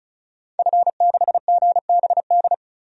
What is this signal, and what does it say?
Morse code, heard as a keyed tone, about 36 wpm, 700 Hz. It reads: F6GBD
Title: DXCC-100 Call Signs - 5 Characters - 35wpm